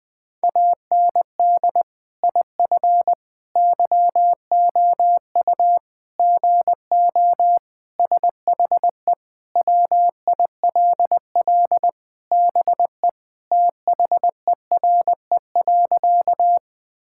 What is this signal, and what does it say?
Morse code, heard as a keyed tone, about 20 wpm, 700 Hz. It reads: AND IF YOU GO SHE WILL BE THERE.
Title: Sentences with 3-Word Phrases - 20wpm